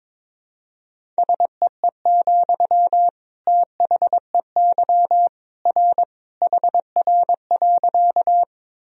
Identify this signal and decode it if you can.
Morse code, heard as a keyed tone, about 22 wpm, 700 Hz. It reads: SEE, THEY R HR.
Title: Sentences from Top 200 Words and Abbreviations 22wpm